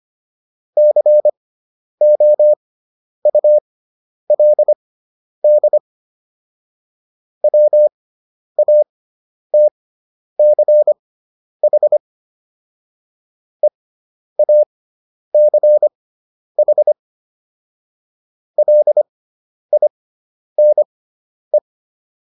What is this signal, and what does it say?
Morse code, heard as a keyed tone, about 25 wpm, spaced as for 10 wpm, 600 Hz. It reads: COULD WATCH EACH LINE